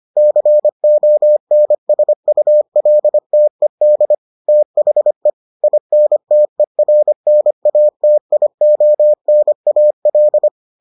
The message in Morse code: CONSULTED THE INTERNATIONAL

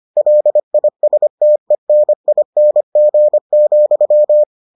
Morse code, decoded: LISTENING,

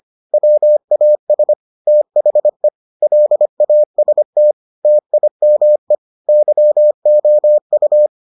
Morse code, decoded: WAS THE LAST TIME YOU

